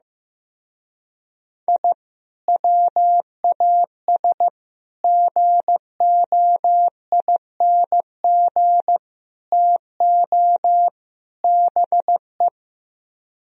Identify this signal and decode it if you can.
Morse code, heard as a keyed tone, about 16 wpm, 700 Hz. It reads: I WAS GOING TO BE
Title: Speed-Racing - Sets of 5 Words 15wpm